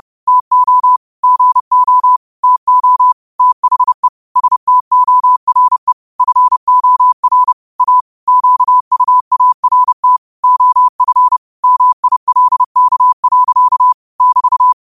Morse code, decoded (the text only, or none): TO GO TO THE STORE FOR A QUART OF MILK. <BT>